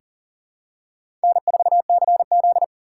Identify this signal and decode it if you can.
Morse code, heard as a keyed tone, about 40 wpm, 700 Hz. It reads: N4CZ
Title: USA Call Signs - 4 Characters - 3x Character Spacing - 40wpm